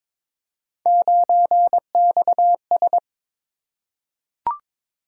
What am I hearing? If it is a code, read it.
Morse code, heard as a keyed tone, about 22 wpm, 700 Hz. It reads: 9XS
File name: practice / 22wpm / Three Letter-Number Combinations - 22wpm